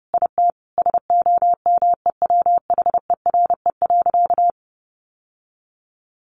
Morse code, heard as a keyed tone, about 30 wpm, 700 Hz. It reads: IT SOMEWHERE.
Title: Sentences from Top 1400 Words 30wpm